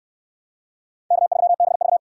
Morse code, heard as a keyed tone, about 69 wpm, 700 Hz. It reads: K3BV